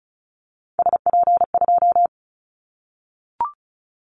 Morse code, heard as a keyed tone, about 36 wpm, 700 Hz. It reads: SP2
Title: Callsign Prefixes for Top 25 DXCC - 35wpm